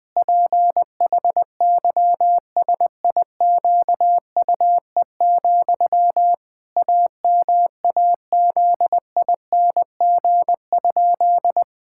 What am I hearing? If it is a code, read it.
Morse code, heard as a keyed tone, about 20 wpm, 700 Hz. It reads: PHYSIQUE, AMAZING?
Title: Pangram Challenge v2 - 20wpm